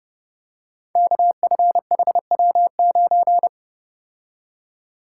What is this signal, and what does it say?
Morse code, heard as a keyed tone, about 28 wpm, 700 Hz. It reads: KFHW9
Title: Code Groups - 5 Letter:Numbers - 30wpm